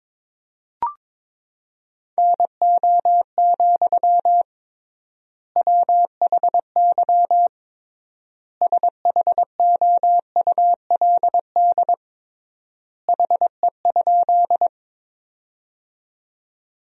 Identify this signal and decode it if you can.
Morse code, heard as a keyed tone, about 22 wpm, 700 Hz. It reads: NO, WHY SHOULD HE?
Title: 3x Word-Spacing - Sentences from Top 300 Words - 22wpm